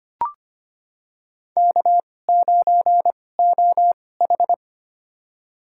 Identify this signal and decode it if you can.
Morse code, heard as a keyed tone, about 25 wpm, spaced as for 18 wpm, 700 Hz. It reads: K9OH
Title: USA Call Signs - 4 Characters - 2x Character Spacing - 25wpm